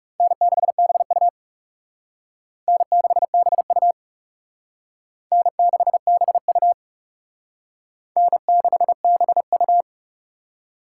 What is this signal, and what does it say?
Morse code, heard as a keyed tone, about 36 wpm, 700 Hz. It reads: N6BU N6BU N6BU N6BU
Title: Speed Racing - USA 1x2 Call Signs 30wpm